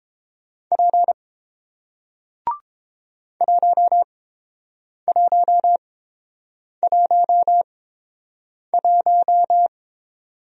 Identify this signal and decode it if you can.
Morse code, heard as a keyed tone, about 30 wpm, spaced as for 7 wpm, 700 Hz. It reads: P 1111